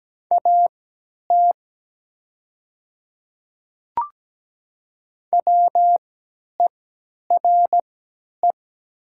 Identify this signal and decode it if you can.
Morse code, heard as a keyed tone, about 17 wpm, spaced as for 10 wpm, 700 Hz. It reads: AT WERE
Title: 3x Character Spacing - Top 100 Words - 17wpm